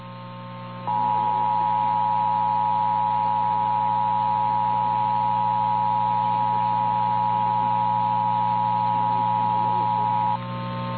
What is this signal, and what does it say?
Emergency Alert System (SAME) tones (the 853 and 960 Hz attention tone), with noise. It is heard from 0.8 to 10.4 s.
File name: eas / KNCYKBIE / KNCYKBIE